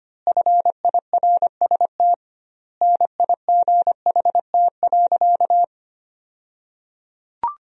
Morse code, heard as a keyed tone, about 25 wpm, 700 Hz. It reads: FIRST NIGHT.
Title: 2x Word-Spacing - Sentences from Top 300 Words - 25wpm